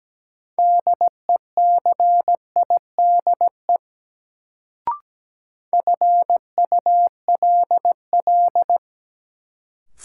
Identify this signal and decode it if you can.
Morse code, heard as a keyed tone, about 17 wpm, 700 Hz. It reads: DECIDE FULL
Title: Sentences from Top 500 Words - New Word Drill - 17wpm